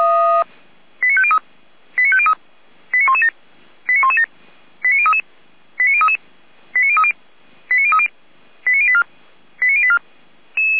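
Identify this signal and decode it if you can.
Five-tone selective call (ZVEI), heard as quick runs of five tones, tones of about 70 ms. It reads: 89503 89503 89117 89117 89033 89033 89030 89030 89084 89084
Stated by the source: SDR receiver recording of ZVEI five-tone selective call, code 89086.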